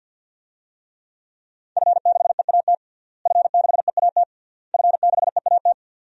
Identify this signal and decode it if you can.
Morse code, heard as a keyed tone, about 47 wpm, 700 Hz. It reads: F6EAT F6EAT F6EAT
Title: DXCC-100 Call Signs - 5 Characters - Repeated 3x - 50wpm